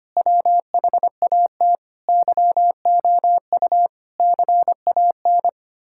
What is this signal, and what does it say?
Morse code, heard as a keyed tone, about 25 wpm, 700 Hz. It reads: WHAT YOU CAN
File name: practice / 25wpm / Sentences with 9-Word Phrases - 25wpm